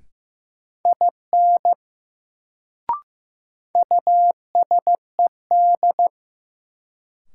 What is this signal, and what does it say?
Morse code, heard as a keyed tone, about 15 wpm, 700 Hz. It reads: IN USED
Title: Sets of 2 Words - New Word Drill - 15wpm